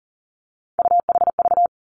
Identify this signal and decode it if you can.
Morse code, heard as a keyed tone, about 38 wpm, 690 Hz. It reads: UHV
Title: Three Challenging Letter-Number Combinations 40wpm